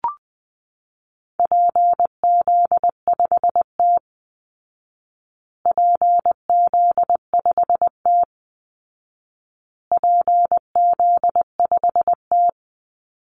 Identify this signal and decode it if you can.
Morse code, heard as a keyed tone, about 19 wpm, 700 Hz. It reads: PZ5T PZ5T PZ5T
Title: Call Signs Repeated 2x - 20wpm